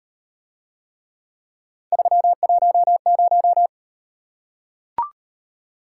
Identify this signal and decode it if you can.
Morse code, heard as a keyed tone, about 36 wpm, 700 Hz. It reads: 310